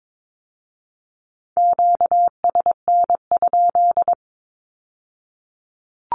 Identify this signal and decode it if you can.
Morse code, heard as a keyed tone, about 22 wpm, 700 Hz. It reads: QSN?
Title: Q-Codes with Spoken Meaning - 22wpm